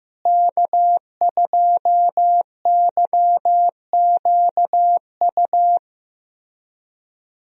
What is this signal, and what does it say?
Morse code, heard as a keyed tone, about 15 wpm, 700 Hz. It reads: K2YQU
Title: USA Call Signs - 5 Characters - Repeated 3x - 15wpm